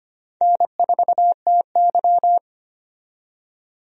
Morse code, heard as a keyed tone, about 25 wpm, 700 Hz. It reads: N4TY
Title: USA Call Signs - 4 Characters - 25wpm